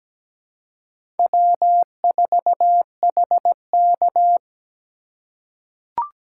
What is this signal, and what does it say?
Morse code, heard as a keyed tone, about 17 wpm, 700 Hz. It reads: W4HK
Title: USA Call Signs - 4 Characters - 6x Character Spacing - 17wpm